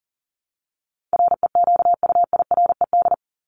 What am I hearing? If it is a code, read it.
Morse code, heard as a keyed tone, about 40 wpm, 700 Hz. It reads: REQUIRED
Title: ED Suffix - Top 100 Words with Base Words 40wpm